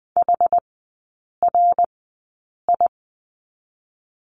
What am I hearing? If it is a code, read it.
Morse code, heard as a keyed tone, about 20 wpm, spaced as for 8 wpm, 690 Hz. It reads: HRI
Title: 060 Lesson 11e - Letter h - Three Characters 20wpm